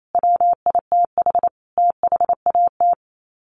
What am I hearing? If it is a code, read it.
Morse code, heard as a keyed tone, about 28 wpm, 700 Hz. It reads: WITH THAT
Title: Sets of 5 Words Encore 28wpm